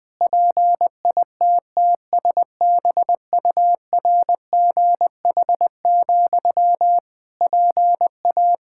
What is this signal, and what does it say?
Morse code, heard as a keyed tone, about 21 wpm, 700 Hz. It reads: PITTSBURGH, PA
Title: Practice QSO - Exchange 1 - Beginning - 20wpm